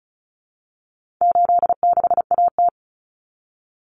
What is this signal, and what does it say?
Morse code, heard as a keyed tone, about 34 wpm, 700 Hz. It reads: 86AT
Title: Common Cut Numbers - 4-Digit 35wpm